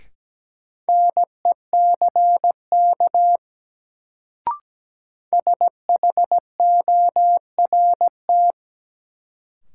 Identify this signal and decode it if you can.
Morse code, heard as a keyed tone, about 17 wpm, 700 Hz. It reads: NECK SHORT
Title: Sets of 5 Words Encore - New Word Drill - 17wpm